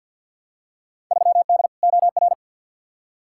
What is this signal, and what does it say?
Morse code, heard as a keyed tone, about 53 wpm, 700 Hz. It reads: 3D OR
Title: Field Day Exchanges Encore - 50wpm